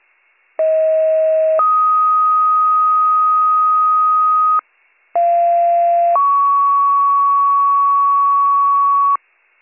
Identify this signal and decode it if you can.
Two-tone page, heard as a one-second tone, then a three-second tone, about 640 then 1250 Hz, about 690 then 1120 Hz. It is heard from 0.6 to 9.2 s.